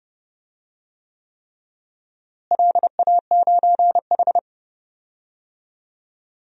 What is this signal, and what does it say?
Morse code, heard as a keyed tone, about 29 wpm, 700 Hz. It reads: LA9H
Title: DXCC-100 Call Signs - 4 Characters - Repeated 3x - 30wpm